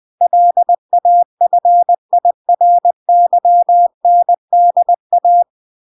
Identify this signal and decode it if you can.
Morse code, heard as a keyed tone, about 20 wpm, 700 Hz. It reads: LAFIRYNDA